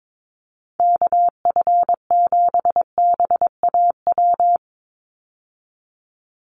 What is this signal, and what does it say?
Morse code, heard as a keyed tone, about 22 wpm, 700 Hz. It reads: KF7BAW
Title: USA Call Signs 22wpm